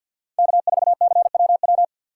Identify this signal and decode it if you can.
Morse code, heard as a keyed tone, about 53 wpm, 700 Hz. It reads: K 4 C W W
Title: USA Call Signs - 6x Character Spacing - 50wpm